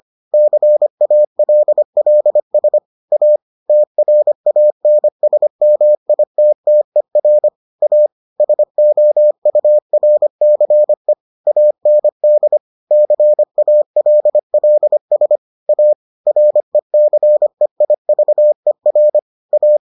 Morse code, heard as a keyed tone, about 25 wpm, 600 Hz. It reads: CALLS A TRANSMITTER A SOURCE AND CALLS A RECEIVER A